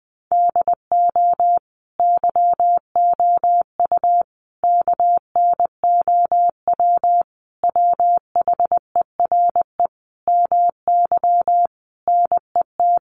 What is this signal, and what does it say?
Morse code, heard as a keyed tone, about 20 wpm, 700 Hz. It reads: DO YOU KNOW WHERE MY NET
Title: Sentences from Top 1400 Words 20wpm